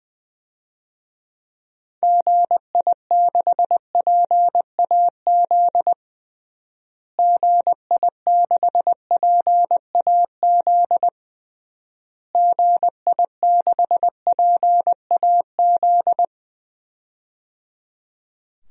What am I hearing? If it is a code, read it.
Morse code, heard as a keyed tone, about 20 wpm, 700 Hz. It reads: GI6PAZ GI6PAZ GI6PAZ
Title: DXCC-100 Call Signs - 6 Characters - Repeated 3x - 20wpm